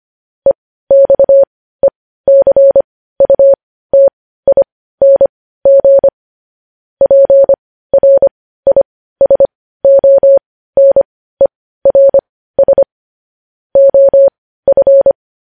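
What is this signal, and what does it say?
Morse code, heard as a keyed tone, about 25 wpm, spaced as for 15 wpm, 550 Hz. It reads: EXECUTING PRISONERS OF
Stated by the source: Morse code news headlines broadcast.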